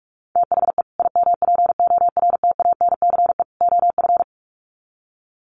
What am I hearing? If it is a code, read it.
Morse code, heard as a keyed tone, about 44 wpm, 700 Hz. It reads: THE IMPORTANCE OF
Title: Sets of 3 Words 45wpm